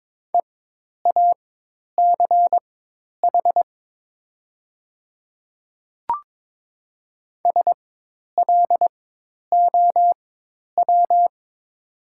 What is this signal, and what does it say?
Morse code, heard as a keyed tone, about 22 wpm, spaced as for 10 wpm, 700 Hz. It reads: EACH SLOW
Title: 4x Character Spacing - Top 500 Words - 22wpm